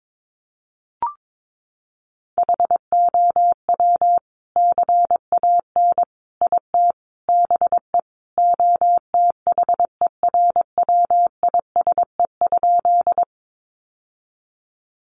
Morse code, heard as a keyed tone, about 22 wpm, 700 Hz. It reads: HOW CAN IT BE OTHERWISE?
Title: Sentences from Top 1200 Words 22wpm